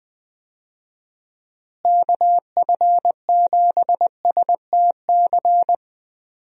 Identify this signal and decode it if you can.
Morse code, heard as a keyed tone, about 20 wpm, 700 Hz. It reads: KF7STC
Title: USA Call Signs - 6 Characters - Repeated 3x - 20wpm